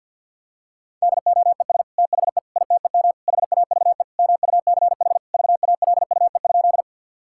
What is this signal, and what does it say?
Morse code, heard as a keyed tone, about 50 wpm, 700 Hz. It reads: DOES THE ITEM HAVE MUCH VALUE?
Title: Sentences from Top 1000 Words - Review Part 6 - 50wpm